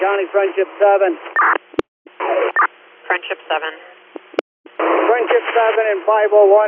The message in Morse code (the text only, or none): none